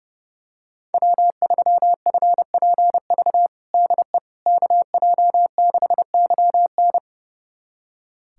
Morse code, heard as a keyed tone, about 31 wpm, 700 Hz. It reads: W3FPV DE KJ6YN
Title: USA Callsign DE Callsign - 5 Characters - 30wpm